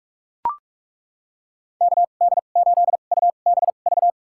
Morse code, heard as a keyed tone, about 42 wpm, spaced as for 31 wpm, 700 Hz. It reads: KD8UBV
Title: USA Call Signs - 2x Character Spacing - 45wpm